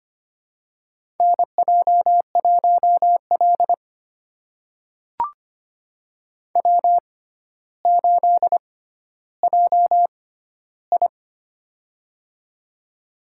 Morse code, heard as a keyed tone, about 25 wpm, spaced as for 13 wpm, 700 Hz. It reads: NJ1L W8JI
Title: USA Call Signs - 4 Characters - 6x Character Spacing - 25wpm